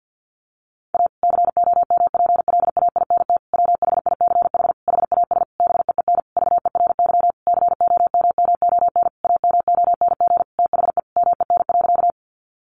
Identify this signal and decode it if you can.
Morse code, heard as a keyed tone, about 50 wpm, 700 Hz. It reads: A COMPLAINT WHICH HAS BEEN VERY COMMON AMONG THE MEN.
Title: Sentences from Top 1400 Words 50wpm